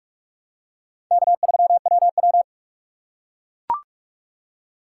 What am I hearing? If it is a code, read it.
Morse code, heard as a keyed tone, about 41 wpm, 700 Hz. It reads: K3WW
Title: USA Call Signs - 4 Characters - 4x Character Spacing - 45wpm